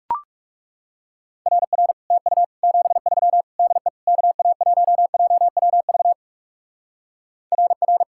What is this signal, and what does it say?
Morse code, heard as a keyed tone, about 37 wpm, 700 Hz. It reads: RR TU 73 DE KA1JWV RR
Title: Speed-Racing Common QSO Phrases Encore - 30wpm